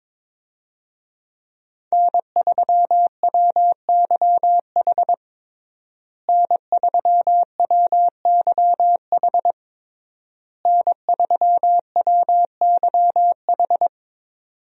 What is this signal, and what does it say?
Morse code, heard as a keyed tone, about 23 wpm, 700 Hz. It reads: N3WYH N3WYH N3WYH
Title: USA Call Signs - 5 Characters - Repeated 3x - 22wpm